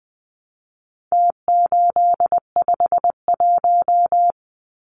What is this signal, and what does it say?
Morse code, heard as a keyed tone, about 20 wpm, 700 Hz. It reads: T851
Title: Numbers - Contest Style - 4-Digit 20wpm